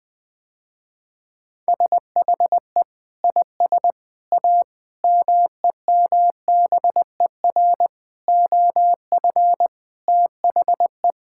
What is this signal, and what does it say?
Morse code, heard as a keyed tone, about 20 wpm, 700 Hz. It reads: SHE IS A MEMBER OF THE
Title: Sentences with 4-Word Phrases - Spoken Beginning - 20wpm